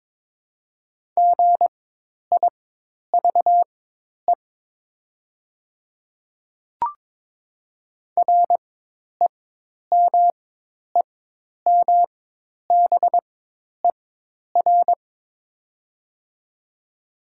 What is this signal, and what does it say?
Morse code, heard as a keyed tone, about 22 wpm, spaced as for 10 wpm, 700 Hz. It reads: GIVE REMEMBER
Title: 4x Character Spacing - Top 500 Words - 22wpm